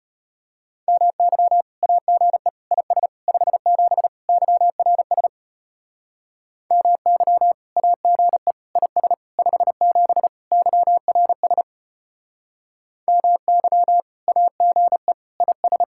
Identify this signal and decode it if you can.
Morse code, heard as a keyed tone, about 32 wpm, 700 Hz. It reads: MY AGE IS 57 YRS MY AGE IS 57 YRS MY AGE IS